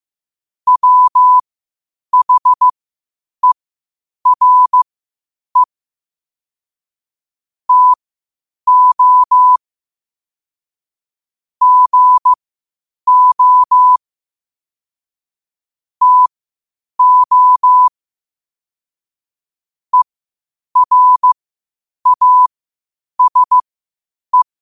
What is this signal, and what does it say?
Morse code, heard as a keyed tone, about 15 wpm, spaced as for 8 wpm, 1000 Hz. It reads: WHERE TO GO TO ERASE
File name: qotdfiles / 8wpm